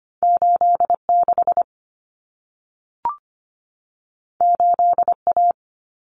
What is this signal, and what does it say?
Morse code, heard as a keyed tone, about 25 wpm, 700 Hz. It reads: 86 8A
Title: Common Cut Numbers - 2-Digit 25wpm